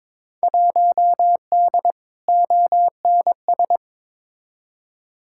Morse code, heard as a keyed tone, about 22 wpm, 700 Hz. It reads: 1D ONS
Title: Field Day Exchanges - 22wpm